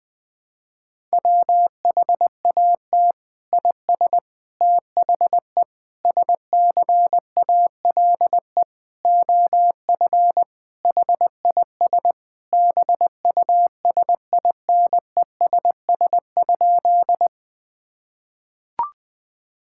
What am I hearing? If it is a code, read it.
Morse code, heard as a keyed tone, about 20 wpm, 700 Hz. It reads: WHAT IS THE SCALE OF HIS BUSINESS?
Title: Sentences from Top 1000 Words - Review Part 5 - 20wpm